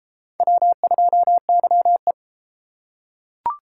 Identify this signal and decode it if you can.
Morse code, heard as a keyed tone, about 30 wpm, 700 Hz. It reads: W2YE